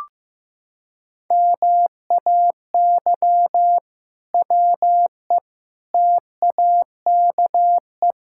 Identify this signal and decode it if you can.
Morse code, heard as a keyed tone, about 15 wpm, 700 Hz. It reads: MAY WE TAKE
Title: Sentences from Top 1600 Words 15wpm